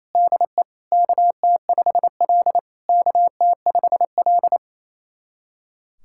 Morse code, heard as a keyed tone, about 27 wpm, 700 Hz. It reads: DE KT5L KT5L
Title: SOTA QSO Practice 28wpm